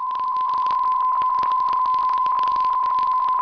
PSK63, about 62.5 baud.
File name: UVB33-2016-06-11-1500utc-12025khz-byW2IBC